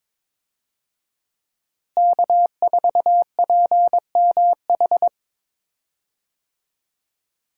Morse code, heard as a keyed tone, about 22 wpm, 700 Hz. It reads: K4PMH
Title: USA Call Signs - 5 Characters - Repeated 3x - 22wpm